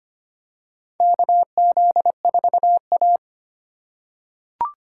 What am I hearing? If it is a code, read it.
Morse code, heard as a keyed tone, about 25 wpm, 700 Hz. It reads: KZ4A
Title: K1USN SST QSO Practice - 25wpm